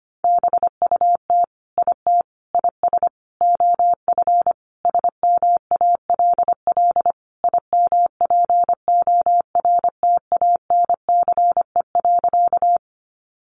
Morse code, heard as a keyed tone, about 25 wpm, 700 Hz. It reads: BUT IT IS OF SMALL IMPORTANCE.
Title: Sentences from Top 1200 Words 25wpm